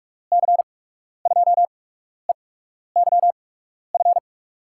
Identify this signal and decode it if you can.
Morse code, heard as a keyed tone, about 42 wpm, spaced as for 12 wpm, 700 Hz. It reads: C2EYF
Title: USA Call Signs - 6 Characters - 8x Character Spacing - 45wpm